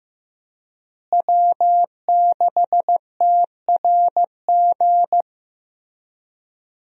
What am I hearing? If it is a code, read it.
Morse code, heard as a keyed tone, about 14 wpm, 700 Hz. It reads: W6TRG
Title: USA Call Signs - 5 Characters - Repeated 2x - 15wpm